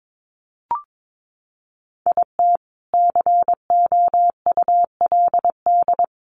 Morse code, heard as a keyed tone, about 22 wpm, 700 Hz. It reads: IT COULD
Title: Sentences from Top 1200 Words 22wpm